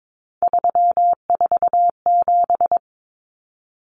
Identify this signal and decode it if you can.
Morse code, heard as a keyed tone, about 22 wpm, 700 Hz. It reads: 347
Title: Numbers - 3-Digit 22wpm